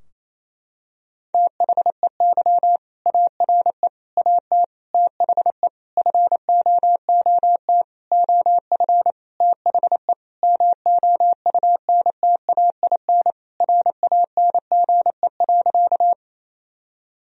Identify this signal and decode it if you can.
Morse code, heard as a keyed tone, about 28 wpm, 700 Hz. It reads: THEY ARE AT THE FOOT OF THE MOUNTAIN RANGE.
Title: Sentences from Top 1000 Words - Review Part 4 - 28wpm